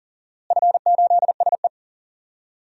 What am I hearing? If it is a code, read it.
Morse code, heard as a keyed tone, about 40 wpm, 700 Hz. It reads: F8SE
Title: DXCC-100 Call Signs - 4 Characters - Repeated 2x - 40wpm